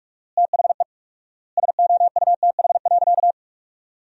Morse code, heard as a keyed tone, about 45 wpm, 700 Hz. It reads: THE SOUTH.
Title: 4x Word-Spacing - Sentences from Top 400 Words - 45wpm